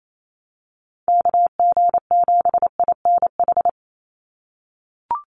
Morse code, heard as a keyed tone, about 28 wpm, 700 Hz. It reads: KG7INH
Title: USA Call Signs 28wpm